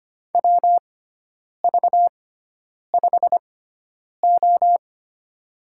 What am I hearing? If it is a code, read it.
Morse code, heard as a keyed tone, about 25 wpm, spaced as for 9 wpm, 700 Hz. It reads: WV5O
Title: USA Call Signs - 6x Character Spacing - 25wpm